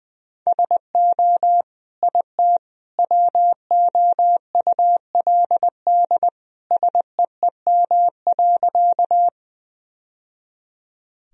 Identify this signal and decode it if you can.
Morse code, heard as a keyed tone, about 21 wpm, 700 Hz. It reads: SO IT WOULD SEEM.
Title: Sentences from Top 1000 Words - Review Part 1 - 20wpm